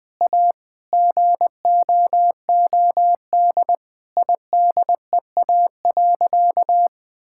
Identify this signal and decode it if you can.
Morse code, heard as a keyed tone, about 20 wpm, 700 Hz. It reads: A GOOD IDEA.
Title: Sentences with 3-Word Phrases - 20wpm